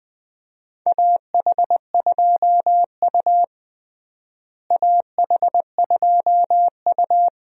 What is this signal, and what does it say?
Morse code, heard as a keyed tone, about 21 wpm, 700 Hz. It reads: AH2U AH2U
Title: USA Call Signs - 4 Characters - Repeated 3x - 20wpm